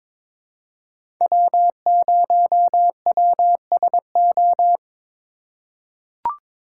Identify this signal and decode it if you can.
Morse code, heard as a keyed tone, about 22 wpm, 700 Hz. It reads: W0WSO
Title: USA Call Signs - 4x Character Spacing - 22wpm